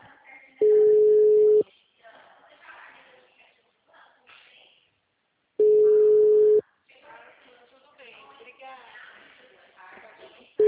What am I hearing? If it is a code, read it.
Ringback tone (Europe, 425 Hz), with speech.
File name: out- -2038- - - .17854